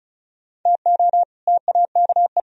Morse code, heard as a keyed tone, about 40 wpm, 700 Hz. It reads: TO TAKE